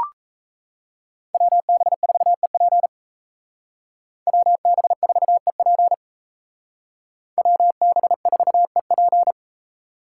Morse code, heard as a keyed tone, about 34 wpm, 700 Hz. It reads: WB4EP WB4EP WB4EP
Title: Speed Racing - USA 2x2 Call Signs 28wpm